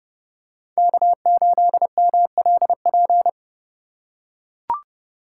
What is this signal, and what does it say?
Morse code, heard as a keyed tone, about 30 wpm, 700 Hz. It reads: K8MLP
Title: USA Call Signs - 5 Characters - 2x Character Spacing - 30wpm